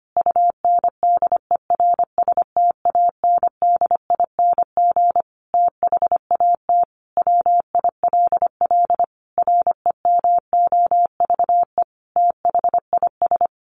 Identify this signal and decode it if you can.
Morse code, heard as a keyed tone, about 25 wpm, 700 Hz. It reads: UNDERSTANDING THAT WILL REMOVE THIS
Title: Sentences from Top 1500 Words 25wpm